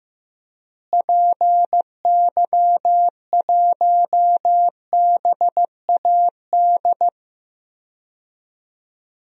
Morse code, heard as a keyed tone, about 15 wpm, 700 Hz. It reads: PY1BAD
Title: DXCC-100 Call Signs - 6 Characters - Repeated 3x - 15wpm